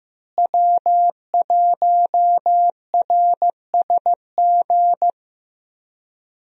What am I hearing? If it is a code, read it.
Morse code, heard as a keyed tone, about 15 wpm, 700 Hz. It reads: W1RSG
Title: USA Call Signs - 5 Characters - Repeated 3x - 15wpm